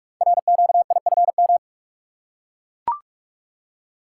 Morse code, heard as a keyed tone, about 44 wpm, 700 Hz. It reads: RQIFM
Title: Code Groups - 5 Letters - 45wpm